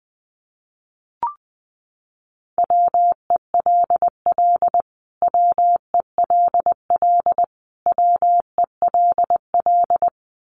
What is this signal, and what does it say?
Morse code, heard as a keyed tone, about 20 wpm, 700 Hz. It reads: WELL WELL WELL
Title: Top 500 Words Repeated 20wpm